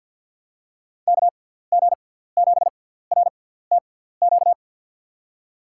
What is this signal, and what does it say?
Morse code, heard as a keyed tone, about 53 wpm, 700 Hz. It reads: K G 7 R T Q
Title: USA Call Signs - 6 Characters - 6x Character Spacing - 50wpm